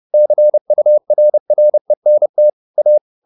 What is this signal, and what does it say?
Morse code, heard as a keyed tone, about 30 wpm, 600 Hz. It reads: CURRENT A
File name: the old man and the sea / the old man and the sea 1 30wpm